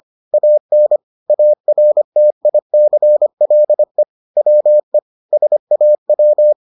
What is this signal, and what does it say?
Morse code, heard as a keyed tone, about 25 wpm, 600 Hz. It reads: AN ARTICLE WE SAW